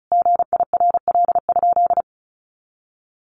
Morse code, heard as a keyed tone, about 34 wpm, 700 Hz. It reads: GIRL?
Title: Sentences from Top 1500 Words 35wpm